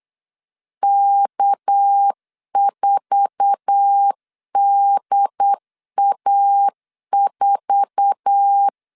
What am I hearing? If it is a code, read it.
Morse code, heard as a keyed tone, about 8 wpm, 800 Hz. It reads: K4DA4
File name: LEZIONI / 8 YJ45 / 8 YJ4 8wpm 3246